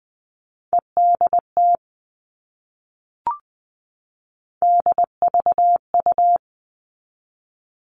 Morse code, heard as a keyed tone, about 20 wpm, 700 Hz. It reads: EDT DVU
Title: All Cut Numbers - 3-Digit 20wpm